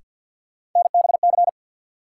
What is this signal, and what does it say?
Morse code, heard as a keyed tone, about 52 wpm, 700 Hz. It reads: NBC
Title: Sets of 5 Words Encore - New Word Drill - 50wpm